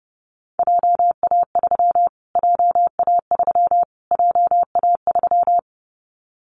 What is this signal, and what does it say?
Morse code, heard as a keyed tone, about 30 wpm, 700 Hz. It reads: JA3 JA3 JA3
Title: Callsign Prefixes for Top 25 DXCC - Repeated 3x - 30wpm